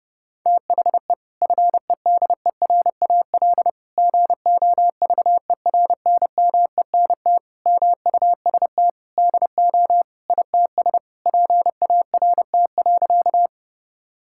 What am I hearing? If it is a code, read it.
Morse code, heard as a keyed tone, about 30 wpm, 700 Hz. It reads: THE FEDERAL GOVERNMENT MUST DO ITS PART.
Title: Sentences from Top 1000 Words - Review Part 5 - 30wpm